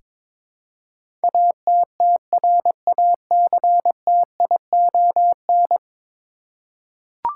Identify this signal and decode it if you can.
Morse code, heard as a keyed tone, about 21 wpm, 700 Hz. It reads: ATTRACTION
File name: practice / 22wpm / 10 Letter Words - First 3 Letters Given - 22wpm